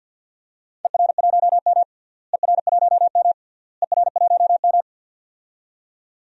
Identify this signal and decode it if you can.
Morse code, heard as a keyed tone, about 49 wpm, 700 Hz. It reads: ER1M ER1M ER1M
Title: DXCC-100 Call Signs - 4 Characters - Repeated 3x - 50wpm